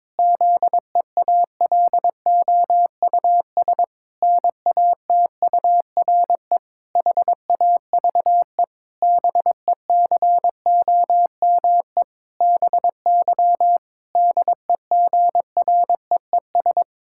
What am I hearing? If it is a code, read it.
Morse code, heard as a keyed tone, about 21 wpm, 700 Hz. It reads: ZEALOUS NATURE HAVE BECOME BY DEGREES
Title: Pangram Challenge v2 - 22wpm